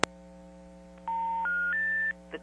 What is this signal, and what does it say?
Special information tone.
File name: TurtleCreekEWSD